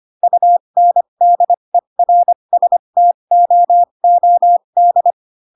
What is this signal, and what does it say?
Morse code, heard as a keyed tone, about 25 wpm, spaced as for 22 wpm, 700 Hz. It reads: UNDERSTOOD